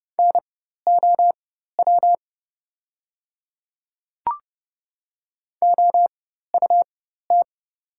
Morse code, heard as a keyed tone, about 30 wpm, spaced as for 14 wpm, 700 Hz. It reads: NOW OUT
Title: 4x Character Spacing - Top 100 Words - 30wpm